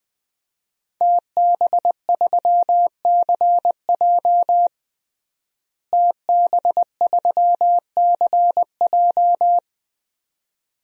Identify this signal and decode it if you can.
Morse code, heard as a keyed tone, about 21 wpm, 700 Hz. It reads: TB3CJ TB3CJ
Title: DXCC-100 Call Signs - 5 Characters - Repeated 2x - 20wpm